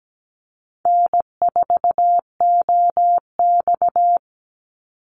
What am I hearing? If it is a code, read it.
Morse code, heard as a keyed tone, about 17 wpm, 700 Hz. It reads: N4OX
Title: Easy Call Signs - 17wpm